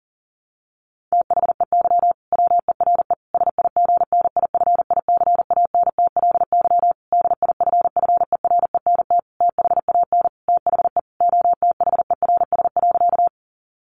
Morse code, heard as a keyed tone, about 37 wpm, 700 Hz. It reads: THEY WERE SIGNIFICANTLY DIFFERENT THAN THE OTHERS.
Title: Sentences from Top 1500 Words 40wpm